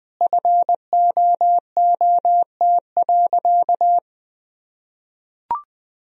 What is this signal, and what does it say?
Morse code, heard as a keyed tone, about 20 wpm, 700 Hz. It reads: FOOT.
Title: Sentences from Top 1000 Words - Review Part 3 - 20wpm